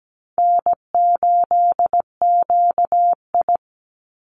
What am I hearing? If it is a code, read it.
Morse code, heard as a keyed tone, about 17 wpm, 700 Hz. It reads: N8QI
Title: USA Call Signs 17wpm